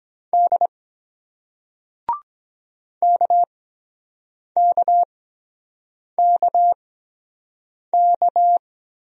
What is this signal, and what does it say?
Morse code, heard as a keyed tone, about 23 wpm, spaced as for 7 wpm, 700 Hz. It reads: D KKKK